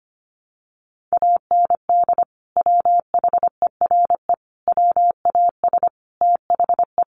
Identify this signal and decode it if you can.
Morse code, heard as a keyed tone, about 25 wpm, 700 Hz. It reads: AND WHERE WAS THE
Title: Sentences from Top 1500 Words 25wpm